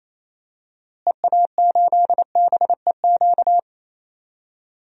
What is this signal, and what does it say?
Morse code, heard as a keyed tone, about 28 wpm, 700 Hz. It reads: EA8BEQ
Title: DXCC-100 Call Signs - 6 Characters - Repeated 2x - 28wpm